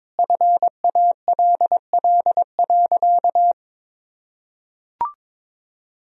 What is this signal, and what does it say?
Morse code, heard as a keyed tone, about 22 wpm, 700 Hz. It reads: FALL.
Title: Sentences from Top 1000 Words - Review Part 2 - 22wpm